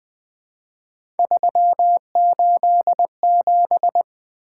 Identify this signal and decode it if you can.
Morse code, heard as a keyed tone, about 20 wpm, 700 Hz. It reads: 387
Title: Three Letter-Number Combinations - 20wpm